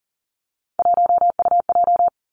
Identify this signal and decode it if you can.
Morse code, heard as a keyed tone, about 42 wpm, 700 Hz. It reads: 1UJ
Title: Three Challenging Letter-Number Combinations 40wpm